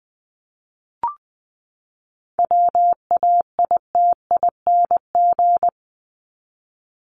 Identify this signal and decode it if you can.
Morse code, heard as a keyed tone, about 20 wpm, 700 Hz. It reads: WAITING
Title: ING Suffix - Top 100 Words with Base Words 20wpm